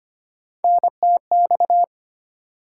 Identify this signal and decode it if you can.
Morse code, heard as a keyed tone, about 25 wpm, 700 Hz. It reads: NTX
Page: morse-code-ninja-practice-25wpm